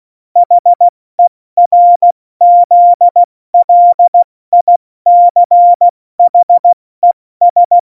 Morse code, heard as a keyed tone, about 16 wpm, 700 Hz. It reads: HERZLICHES
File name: Morsekurs / Witze / Witz 21 Juli 2020, 70 F80